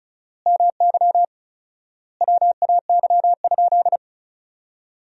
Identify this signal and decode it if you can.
Morse code, heard as a keyed tone, about 36 wpm, 700 Hz. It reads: MY WAY?
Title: 4x Word-Spacing - Sentences from Top 200 Words - 35wpm